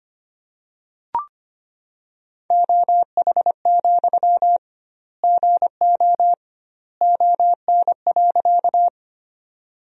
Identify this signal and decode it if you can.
Morse code, heard as a keyed tone, about 25 wpm, 700 Hz. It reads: OH, GO ON.
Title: 2x Word-Spacing - Sentences from Top 500 Words - 25wpm